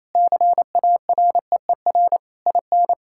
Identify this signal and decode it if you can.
Morse code, heard as a keyed tone, about 28 wpm, 690 Hz. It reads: CAREER IN
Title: Sentences from Top 1000 Words - Review Part 6 - 28wpm